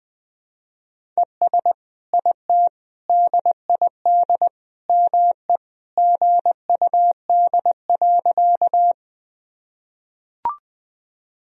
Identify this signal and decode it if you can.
Morse code, heard as a keyed tone, about 20 wpm, 700 Hz. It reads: ES IT DID ME GUD.
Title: Sentences from Top 200 Words and Abbreviations 20wpm